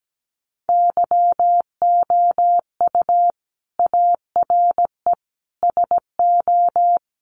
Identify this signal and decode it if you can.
Morse code, heard as a keyed tone, about 17 wpm, 700 Hz. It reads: YOU ARE SO
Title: Sentences from Top 1200 Words 17wpm